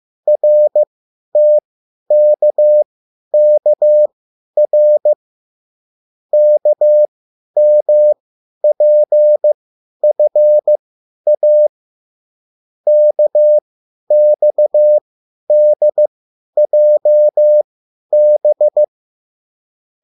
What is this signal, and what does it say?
Morse code, heard as a keyed tone, about 15 wpm, spaced as for 10 wpm, 600 Hz. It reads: RTKKR KMPFA KXDJB